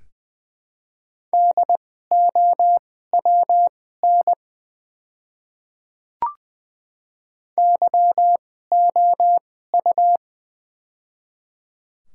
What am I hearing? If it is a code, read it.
Morse code, heard as a keyed tone, about 20 wpm, spaced as for 15 wpm, 700 Hz. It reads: DOWN YOU
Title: 2x Character Spacing - Top 100 Words - 20wpm